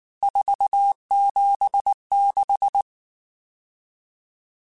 Morse code, heard as a keyed tone, about 19 wpm, 800 Hz. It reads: 476
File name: RSTs / RST-mp3s / Various RST @18